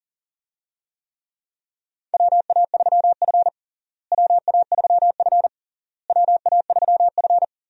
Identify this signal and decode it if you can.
Morse code, heard as a keyed tone, about 40 wpm, 700 Hz. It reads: WA3F WA3F WA3F
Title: USA Call Signs - 4 Characters - Repeated 3x - 40wpm